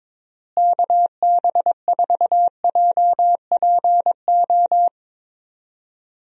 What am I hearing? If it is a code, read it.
Morse code, heard as a keyed tone, about 22 wpm, 700 Hz. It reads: KB4JPO
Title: USA Call Signs - 6 Characters - Repeated 3x - 22wpm